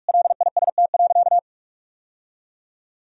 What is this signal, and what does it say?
Morse code, heard as a keyed tone, about 41 wpm, 700 Hz. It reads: LIST.
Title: Sentences from Top 1000 Words - Review Part 2 - 45wpm